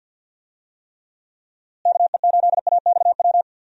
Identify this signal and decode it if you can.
Morse code, heard as a keyed tone, about 52 wpm, 700 Hz. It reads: KE8AXW
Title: USA Call Signs - 6 Characters - Repeated 3x - 50wpm